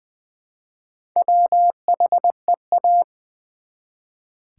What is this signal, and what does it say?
Morse code, heard as a keyed tone, about 20 wpm, 700 Hz. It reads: WHEA
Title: Build A Word - 1 Letter At A Time - 3 to 5-Letter Words - 20wpm